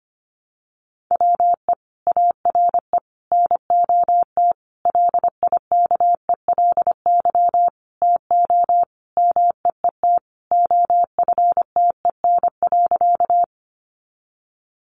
Morse code, heard as a keyed tone, about 25 wpm, 700 Hz. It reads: WE ARE NOT LIKELY TO MEET OFTEN.
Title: Sentences from Top 1100 Words 25wpm